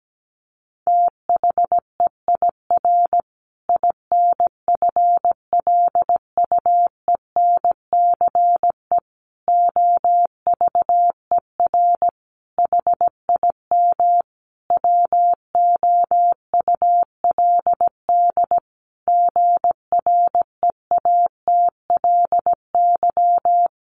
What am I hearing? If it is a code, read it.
Morse code, heard as a keyed tone, about 17 wpm, 700 Hz. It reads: THEIR INFLUENCE OVER HIM WOULD GREATLY
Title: Sentences from Top 1400 Words 17wpm